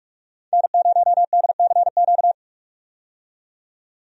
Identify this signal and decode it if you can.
Morse code, heard as a keyed tone, about 46 wpm, 700 Hz. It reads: N0DCQ
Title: USA Call Signs - 5 Characters - 45wpm